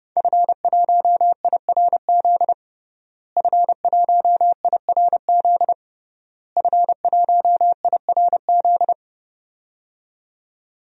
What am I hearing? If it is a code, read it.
Morse code, heard as a keyed tone, about 30 wpm, 700 Hz. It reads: F1IRZ F1IRZ F1IRZ
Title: DXCC-100 Call Signs - 5 Characters - Repeated 3x - 30wpm